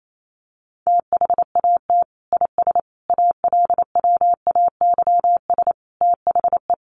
Morse code, heard as a keyed tone, about 28 wpm, 700 Hz. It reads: THAT IS ALWAYS THE
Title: Sentences from Top 700 Words 28wpm